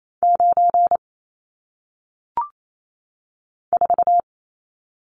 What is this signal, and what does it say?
Morse code, heard as a keyed tone, about 28 wpm, 700 Hz. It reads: 9 4
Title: Numbers - 1-Digit 28wpm